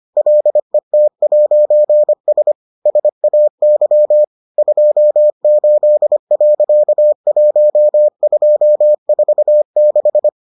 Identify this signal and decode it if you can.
Morse code, heard as a keyed tone, about 25 wpm, 600 Hz. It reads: LET'S SAY 28.1246